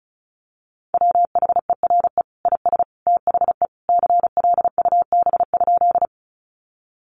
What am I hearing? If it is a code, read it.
Morse code, heard as a keyed tone, about 34 wpm, 700 Hz. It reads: WHERE IS THE CLUB?
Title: Sentences from Top 1200 Words 35wpm